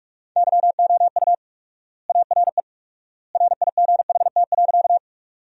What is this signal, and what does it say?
Morse code, heard as a keyed tone, about 45 wpm, 700 Hz. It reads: YOU ARE RIGHT.
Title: 4x Word-Spacing - Sentences from Top 200 Words - 45wpm